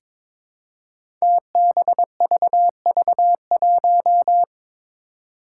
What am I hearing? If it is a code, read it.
Morse code, heard as a keyed tone, about 22 wpm, 700 Hz. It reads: TBVV1
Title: Five Challenging Letter-Number Combinations 22wpm